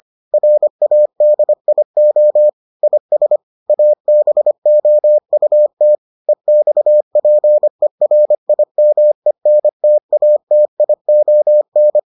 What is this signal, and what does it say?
Morse code, heard as a keyed tone, about 25 wpm, 600 Hz. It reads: RADIO IS ABOUT EXPERIMENTATION